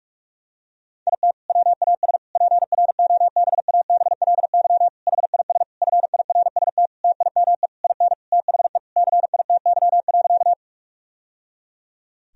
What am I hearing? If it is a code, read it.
Morse code, heard as a keyed tone, about 42 wpm, 700 Hz. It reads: IT WAS PROBABLY HIS FIRST TIME IN THE CITY.
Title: Sentences from Top 1000 Words - Review Part 4 - 45wpm